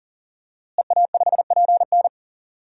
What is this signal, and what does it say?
Morse code, heard as a keyed tone, about 40 wpm, 700 Hz. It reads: EA5PN
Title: DXCC-100 Call Signs - 5 Characters - Repeated 2x - 40wpm